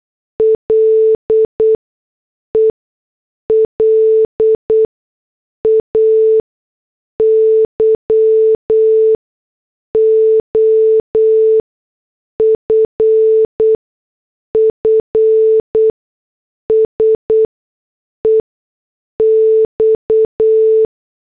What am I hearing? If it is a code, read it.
Morse code, heard as a keyed tone, about 8 wpm, spaced as for 6 wpm, 440 Hz. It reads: LE LAYOFFS EX